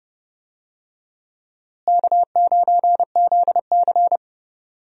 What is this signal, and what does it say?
Morse code, heard as a keyed tone, about 30 wpm, 700 Hz. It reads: K9ZC
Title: USA Call Signs - 4 Characters - Repeated 3x - 30wpm